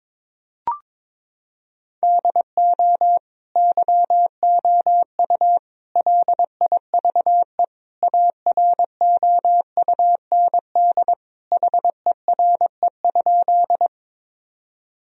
Morse code, heard as a keyed tone, about 22 wpm, 700 Hz. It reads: DO YOU LIVE AROUND HERE?
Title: Sentences from Top 1000 Words - Review Part 3 - 22wpm